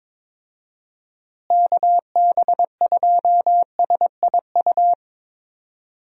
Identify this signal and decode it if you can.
Morse code, heard as a keyed tone, about 21 wpm, 700 Hz. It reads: KB2SIU
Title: USA Call Signs - 6 Characters - Repeated 2x - 22wpm